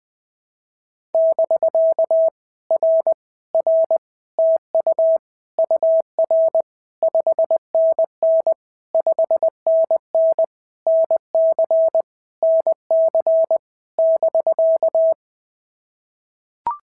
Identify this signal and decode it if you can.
Morse code, heard as a keyed tone, about 20 wpm, 650 Hz. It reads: <BK> R R TU UR 5NN 5NN NC NC <BK>
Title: POTA QSO Practice - 20wpm